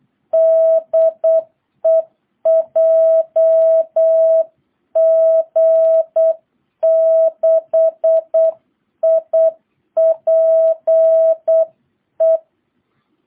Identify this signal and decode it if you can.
Morse code, heard as a keyed tone, about 8 wpm, 660 Hz. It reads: DE JG6IPE